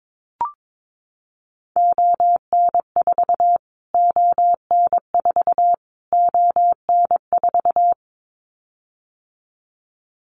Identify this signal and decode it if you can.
Morse code, heard as a keyed tone, about 22 wpm, 700 Hz. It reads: ON4 ON4 ON4
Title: Callsign Prefixes for Top 25 DXCC - Repeated 3x - 22wpm